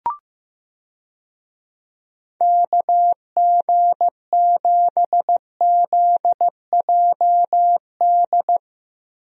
Morse code, heard as a keyed tone, about 15 wpm, 700 Hz. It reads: KG7ZJD
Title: USA Call Signs - 6 Characters - Repeated 3x - 15wpm